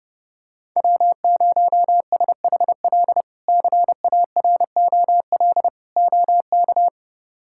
Morse code, heard as a keyed tone, about 30 wpm, 700 Hz. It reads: W0SHL CAROL OK
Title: K1USN SST QSO Practice - 30wpm